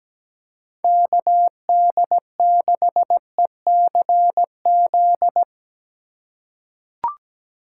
Morse code, heard as a keyed tone, about 17 wpm, 700 Hz. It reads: KD6ECZ
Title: USA Call Signs - 3x Character Spacing - 17wpm